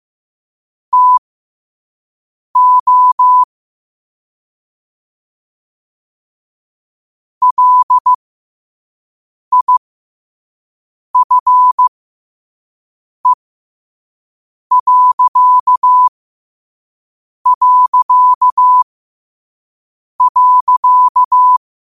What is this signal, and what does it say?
Morse code, heard as a keyed tone, about 15 wpm, spaced as for 5 wpm, 1000 Hz. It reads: TO LIFE...
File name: qotdfiles / 5wpm